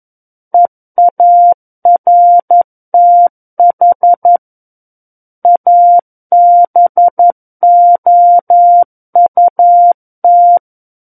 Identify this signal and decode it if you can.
Morse code, heard as a keyed tone, about 11 wpm, 700 Hz. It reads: EARTH ABOUT
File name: Book One / 02-Chapter 2 - 11 WPM